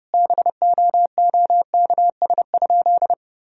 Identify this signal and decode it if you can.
Morse code, heard as a keyed tone, about 32 wpm, 700 Hz. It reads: BOOKS?
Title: Sentences from Top 1000 Words - Review Part 4 - 30wpm